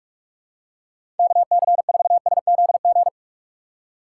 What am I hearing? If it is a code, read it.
Morse code, heard as a keyed tone, about 46 wpm, 700 Hz. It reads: KC4SZG
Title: USA Call Signs - 6 Characters - 3x Character Spacing - 45wpm